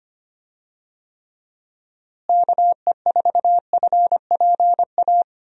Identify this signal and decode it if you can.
Morse code, heard as a keyed tone, about 25 wpm, 700 Hz. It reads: KE4FPA
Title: USA Call Signs - 6 Characters - Repeated 3x - 25wpm